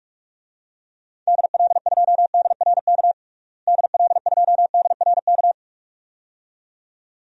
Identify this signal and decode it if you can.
Morse code, heard as a keyed tone, about 46 wpm, 700 Hz. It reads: DL2DRK DL2DRK
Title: DXCC-100 Call Signs - 6 Characters - Repeated 2x - 45wpm